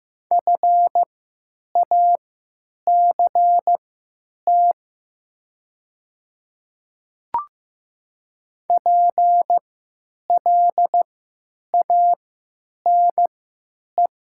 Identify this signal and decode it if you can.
Morse code, heard as a keyed tone, about 15 wpm, spaced as for 8 wpm, 700 Hz. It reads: FACT PLANE
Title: 3x Character Spacing - Top 500 Words - 15wpm